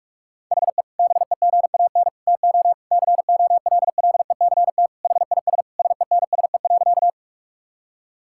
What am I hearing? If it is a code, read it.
Morse code, heard as a keyed tone, about 46 wpm, 700 Hz. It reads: HE BEGAN TO COLLECT HIS SENSE.
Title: Sentences from Top 1600 Words 45wpm